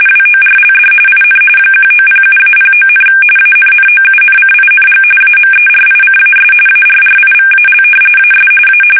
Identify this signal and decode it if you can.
RTTY, about 45.45 baud. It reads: WORLD IN 1907-09, ANNOUNCING AMERICAS ARRIVAL AS A
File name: fsk850 / battleshipnj